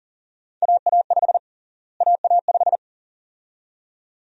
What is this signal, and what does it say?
Morse code, heard as a keyed tone, about 40 wpm, 700 Hz. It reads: AA5 AA5
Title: USA Call Sign Prefixes - Repeated 2x - 40wpm